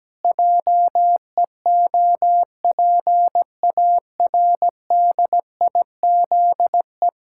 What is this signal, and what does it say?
Morse code, heard as a keyed tone, about 17 wpm, 700 Hz. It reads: JEOPARDIZE
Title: Pangram Challenge v2 - 17wpm